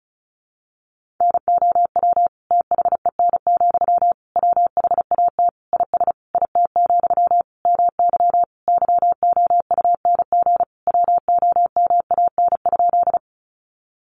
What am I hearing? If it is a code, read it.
Morse code, heard as a keyed tone, about 34 wpm, 700 Hz. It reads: NOW THEN, WHAT IS IT, MY YOUNG WOMAN?
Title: Sentences from Top 1100 Words 35wpm